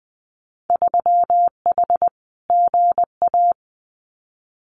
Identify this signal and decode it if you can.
Morse code, heard as a keyed tone, about 20 wpm, 700 Hz. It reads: 3H GA
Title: Winter Field Day Exchanges Encore 20wpm